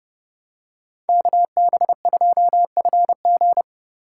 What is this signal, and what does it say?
Morse code, heard as a keyed tone, about 30 wpm, 700 Hz. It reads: KB2FG